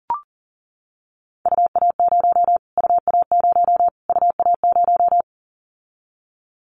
Morse code, heard as a keyed tone, about 40 wpm, 700 Hz. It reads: UA0 UA0 UA0
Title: Callsign Prefixes for Top 25 DXCC - Repeated 3x - 40wpm